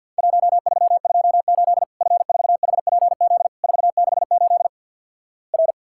Morse code, heard as a keyed tone, about 50 wpm, 700 Hz. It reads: 1328 F4HPZ 468 R
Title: Worked All Europe DX Contest - QTC Exchange - 50wpm